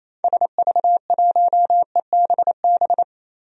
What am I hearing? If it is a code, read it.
Morse code, heard as a keyed tone, about 28 wpm, 700 Hz. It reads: SV1EBB
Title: DXCC-100 Call Signs - 6 Characters - 28wpm